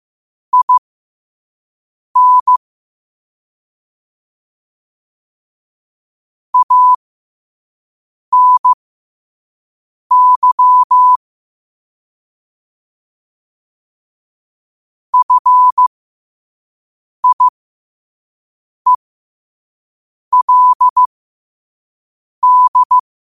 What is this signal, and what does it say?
Morse code, heard as a keyed tone, about 15 wpm, spaced as for 5 wpm, 1000 Hz. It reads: IN ANY FIELD
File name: 5wpm